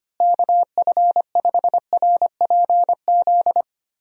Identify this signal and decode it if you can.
Morse code, heard as a keyed tone, about 25 wpm, 700 Hz. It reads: KF5RPZ
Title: USA Call Signs - 6 Characters - Repeated 2x - 25wpm